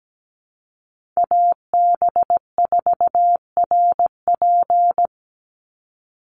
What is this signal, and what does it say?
Morse code, heard as a keyed tone, about 17 wpm, 700 Hz. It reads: AB4RP
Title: USA Call Signs 17wpm